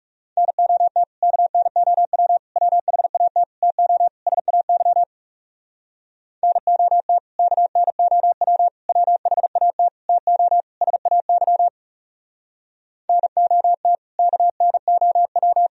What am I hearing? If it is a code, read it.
Morse code, heard as a keyed tone, about 38 wpm, 700 Hz. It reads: NOT KNOW WHAT TO SAY NOT KNOW WHAT TO SAY NOT KNOW
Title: Speed-Racing - Sets of 5 Words 30wpm